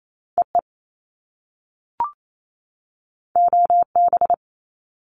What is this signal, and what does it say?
Morse code, heard as a keyed tone, about 28 wpm, 700 Hz. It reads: EE OB
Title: Top Two Letter Combinations in Words 28wpm